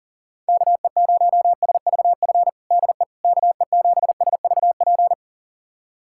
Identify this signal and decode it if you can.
Morse code, heard as a keyed tone, about 40 wpm, 700 Hz. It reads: KE0SVF DE KE7SVP
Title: USA Callsign DE Callsign - 6 Characters - 40wpm